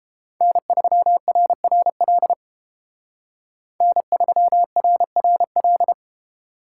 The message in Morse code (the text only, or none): N3RRL N3RRL